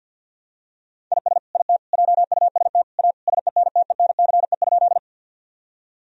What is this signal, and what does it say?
Morse code, heard as a keyed tone, about 49 wpm, 700 Hz. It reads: IS IT JUST A SENTENCE?
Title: Sentences from Top 1000 Words - Review Part 1 - 50wpm